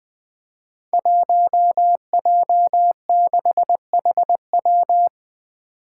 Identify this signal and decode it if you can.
Morse code, heard as a keyed tone, about 20 wpm, 700 Hz. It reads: 1J6HW
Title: Five Challenging Letter-Number Combinations 20wpm